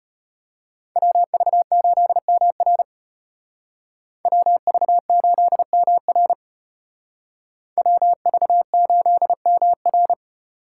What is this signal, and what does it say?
Morse code, heard as a keyed tone, about 32 wpm, 700 Hz. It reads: WV8MR WV8MR WV8MR